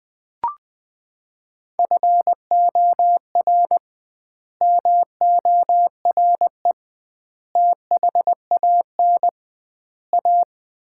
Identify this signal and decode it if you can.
Morse code, heard as a keyed tone, about 21 wpm, 700 Hz. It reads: FOR MORE THAN A
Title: 2x Word-Spacing - Sets of 5 Words - 20wpm